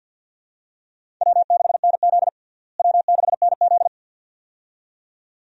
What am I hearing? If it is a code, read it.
Morse code, heard as a keyed tone, about 52 wpm, 700 Hz. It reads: W6NZ W6NZ
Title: USA Call Signs - 4 Characters - Repeated 2x - 50wpm